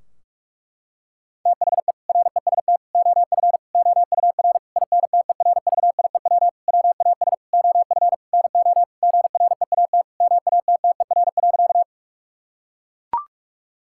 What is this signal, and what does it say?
Morse code, heard as a keyed tone, about 41 wpm, 700 Hz. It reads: THE REST OF OUR INTERVIEW WAS OF NO GREAT MATTER.
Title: Sentences from Top 1600 Words 45wpm